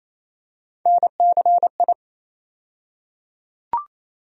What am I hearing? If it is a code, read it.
Morse code, heard as a keyed tone, about 28 wpm, 700 Hz. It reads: NCI
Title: USA Call Sign Postfixes - 3 Letters 28wpm